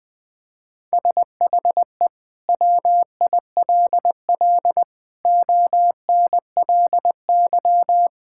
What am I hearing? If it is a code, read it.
Morse code, heard as a keyed tone, about 20 wpm, 700 Hz. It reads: SHE WILL ONLY
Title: Sentences from Top 1000 Words - Review Part 5 - 20wpm